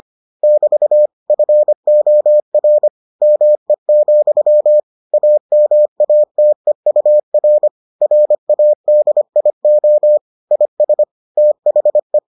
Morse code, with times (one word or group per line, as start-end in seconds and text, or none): 0.4-1.1 <BT>
1.3-2.9 FOR
3.2-4.8 ME,
5.1-7.7 AMATEUR
8.0-10.2 RADIO
10.5-11.0 IS
11.4-12.2 THE